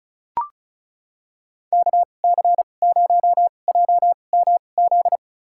Morse code, heard as a keyed tone, about 34 wpm, spaced as for 25 wpm, 700 Hz. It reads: KC0JMZ
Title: USA Call Signs - 6 Characters - 2x Character Spacing - 35wpm